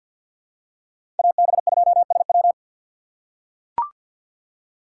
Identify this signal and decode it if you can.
Morse code, heard as a keyed tone, about 46 wpm, 700 Hz. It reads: AB2SW
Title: USA Call Signs - 5 Characters - 6x Character Spacing - 50wpm